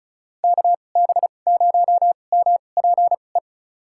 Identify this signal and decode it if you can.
Morse code, heard as a keyed tone, about 36 wpm, spaced as for 25 wpm, 700 Hz. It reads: KB0MPE
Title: USA Call Signs - 6 Characters - 2x Character Spacing - 35wpm